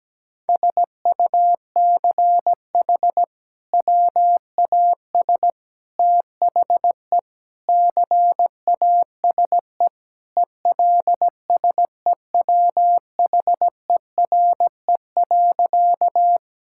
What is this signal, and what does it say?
Morse code, heard as a keyed tone, about 17 wpm, 700 Hz. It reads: SUCH WAS THE CASE ELSEWHERE.
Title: Sentences from Top 1600 Words 17wpm